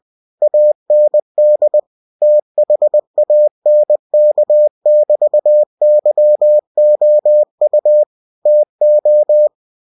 Morse code, heard as a keyed tone, about 20 wpm, 600 Hz. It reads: AND THANK<BT>YOU TO
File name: cw-books / winnie-the-pooh / winnie-20wpm-page-0109